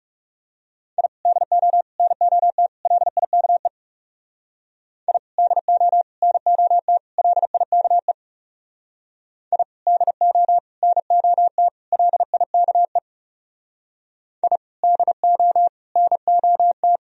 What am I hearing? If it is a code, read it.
Morse code, heard as a keyed tone, about 36 wpm, 700 Hz. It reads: I DO NOT LIKE I DO NOT LIKE I DO NOT LIKE I DO NOT
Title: Speed-Racing - Sets of 4 Words 30wpm